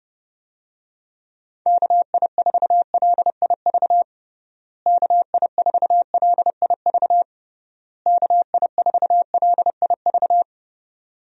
Morse code, heard as a keyed tone, about 30 wpm, 700 Hz. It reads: KI4LIV KI4LIV KI4LIV
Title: USA Call Signs - 6 Characters - Repeated 3x - 30wpm